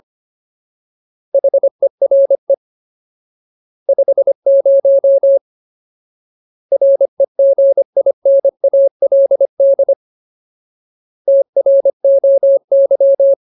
Morse code, heard as a keyed tone, about 25 wpm, 550 Hz. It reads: HERE 50 REGINALD TROY
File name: Ultimate Word List / 25WPM Ultimate